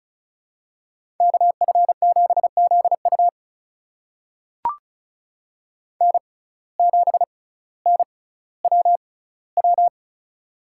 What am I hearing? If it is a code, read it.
Morse code, heard as a keyed tone, about 34 wpm, spaced as for 18 wpm, 700 Hz. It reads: KF7ZU N7NWW
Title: USA Call Signs - 5 Characters - 6x Character Spacing - 35wpm